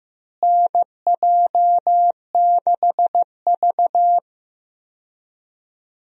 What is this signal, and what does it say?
Morse code, heard as a keyed tone, about 15 wpm, 700 Hz. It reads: NJ6V
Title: USA Call Signs - 4 Characters - Repeated 2x - 15wpm